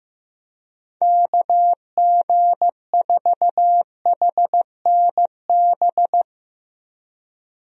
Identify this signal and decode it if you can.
Morse code, heard as a keyed tone, about 15 wpm, 700 Hz. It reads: KG4HNB
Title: USA Call Signs - 6 Characters - Repeated 3x - 15wpm